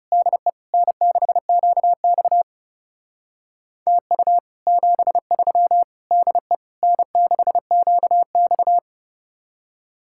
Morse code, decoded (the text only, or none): DE N6QX TU 73 DE N6QX